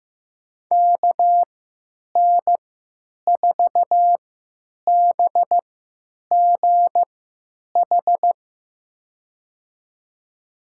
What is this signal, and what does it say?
Morse code, heard as a keyed tone, about 15 wpm, spaced as for 9 wpm, 700 Hz. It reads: KN4BGH
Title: USA Call Signs - 6 Characters - 3x Character Spacing - 15wpm